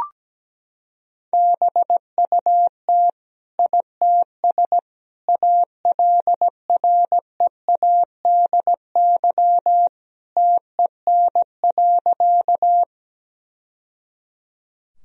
Morse code, heard as a keyed tone, about 17 wpm, 700 Hz. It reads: BUT ITS ALREADY TEN.
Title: Sentences from Top 1000 Words - Review Part 2 - 17wpm